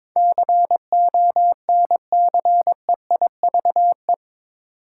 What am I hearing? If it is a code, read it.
Morse code, heard as a keyed tone, about 22 wpm, 700 Hz. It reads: CONCEIVE
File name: practice / 22wpm / Eight Letter Words - 22wpm